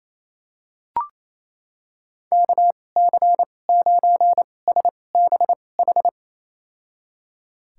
Morse code, heard as a keyed tone, about 28 wpm, spaced as for 20 wpm, 700 Hz. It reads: KC9SBH
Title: USA Call Signs - 6 Characters - 2x Character Spacing - 28wpm